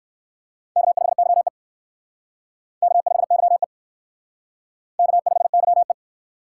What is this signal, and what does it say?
Morse code, heard as a keyed tone, about 54 wpm, 700 Hz. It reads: K5YE K5YE K5YE
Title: Speed Racing - USA 1x2 Call Signs 45wpm